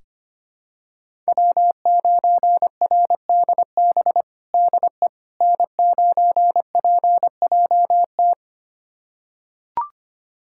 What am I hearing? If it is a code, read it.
Morse code, heard as a keyed tone, about 25 wpm, 700 Hz. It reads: W9RDB DE N9PJT
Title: USA Callsign DE Callsign - 5 Characters - 25wpm